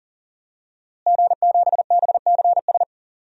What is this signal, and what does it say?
Morse code, heard as a keyed tone, about 40 wpm, 700 Hz. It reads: G7BCS
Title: DXCC-100 Call Signs - 5 Characters - Repeated 3x - 40wpm